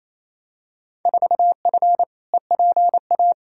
Morse code, heard as a keyed tone, about 28 wpm, 700 Hz. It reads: 4F EPA
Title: Field Day Exchanges Encore - 28wpm